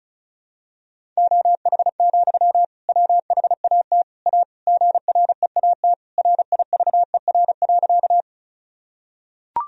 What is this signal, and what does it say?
Morse code, heard as a keyed tone, about 36 wpm, 700 Hz. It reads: OH, WHAT A GREAT RIVER.
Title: Sentences from Top 1000 Words - Review Part 1 - 35wpm